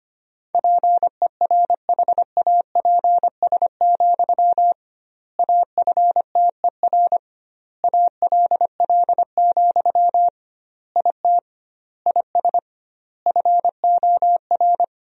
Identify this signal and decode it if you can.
Morse code, heard as a keyed tone, about 25 wpm, 700 Hz. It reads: PERHAPS, AFTER ALL, IT IS FOR
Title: 2x Word-Spacing - Sentences from Top 500 Words - 25wpm